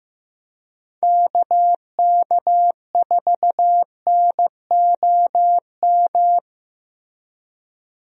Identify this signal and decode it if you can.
Morse code, heard as a keyed tone, about 15 wpm, 700 Hz. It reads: KK4NOM
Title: USA Call Signs - 6 Characters - Repeated 3x - 15wpm